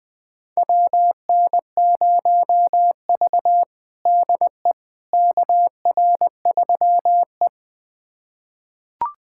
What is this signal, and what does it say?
Morse code, heard as a keyed tone, about 20 wpm, 700 Hz. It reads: WN0V DE KR3E
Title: USA Callsign DE Callsign - 4 Characters - 20wpm